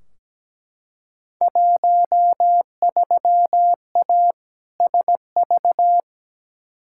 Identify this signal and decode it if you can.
Morse code, heard as a keyed tone, about 17 wpm, 700 Hz. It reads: 13A SV
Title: Field Day Exchanges Encore - 17wpm